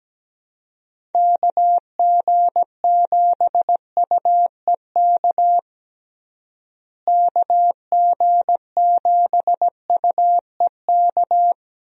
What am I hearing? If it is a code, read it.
Morse code, heard as a keyed tone, about 17 wpm, 700 Hz. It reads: KG7UEK KG7UEK
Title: USA Call Signs - 6 Characters - Repeated 3x - 17wpm